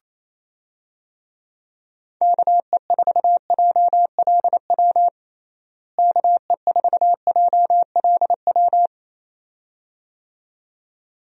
Morse code, heard as a keyed tone, about 28 wpm, 700 Hz. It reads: KE4JLW KE4JLW
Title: USA Call Signs - 6 Characters - Repeated 2x - 28wpm